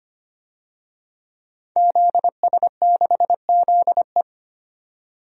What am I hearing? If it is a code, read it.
Morse code, heard as a keyed tone, about 25 wpm, 700 Hz. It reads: ZS6ZE
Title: DXCC-100 Call Signs - 5 Characters - 25wpm